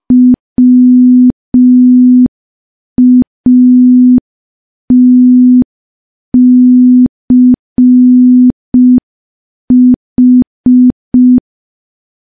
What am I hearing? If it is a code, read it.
Morse code, heard as a keyed tone, about 5 wpm, 250 Hz. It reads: WATCH
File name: morse / morse-training-05WPM